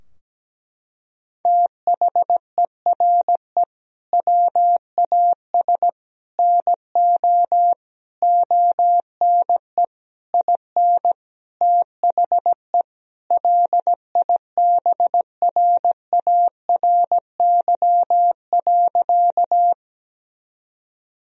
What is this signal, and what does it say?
Morse code, heard as a keyed tone, about 17 wpm, 700 Hz. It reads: THERE WAS NO ONE IN THE LIBRARY.
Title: Sentences from Top 1000 Words - Review Part 4 - 17wpm